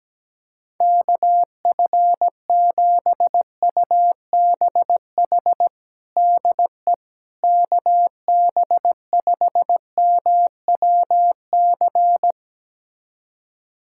Morse code, heard as a keyed tone, about 17 wpm, 700 Hz. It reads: KF7UBH DE KB5MWC
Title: USA Callsign DE Callsign - 6 Characters - 17wpm